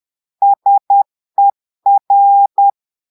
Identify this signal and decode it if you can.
Morse code, heard as a keyed tone, about 10 wpm, 800 Hz. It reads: SER
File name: portugais texte en clair 2 vitesse 10 mots minutes